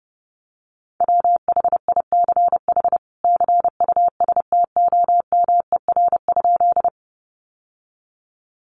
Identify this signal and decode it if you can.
Morse code, heard as a keyed tone, about 30 wpm, 700 Hz. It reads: WHICH CUSTOMER?
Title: Sentences from Top 1100 Words 30wpm